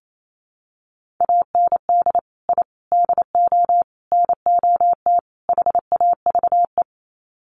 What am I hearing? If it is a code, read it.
Morse code, heard as a keyed tone, about 28 wpm, 700 Hz. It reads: AND I DO NOT HAVE
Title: Sets of 5 Words Encore 28wpm